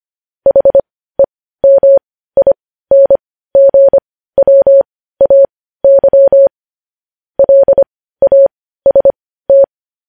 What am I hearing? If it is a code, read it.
Morse code, heard as a keyed tone, about 25 wpm, spaced as for 15 wpm, 550 Hz. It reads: HEMINGWAY LAST
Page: news-headlines-in-morse-code-at-15-wpm